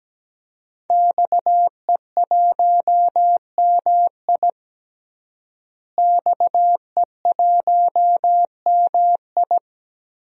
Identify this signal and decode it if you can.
Morse code, heard as a keyed tone, about 17 wpm, 700 Hz. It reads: XE1MI XE1MI
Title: DXCC-100 Call Signs - 5 Characters - Repeated 2x - 17wpm